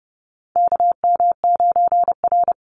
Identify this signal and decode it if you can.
Morse code, heard as a keyed tone, about 30 wpm, 700 Hz. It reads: KM9R
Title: Call Signs - 30wpm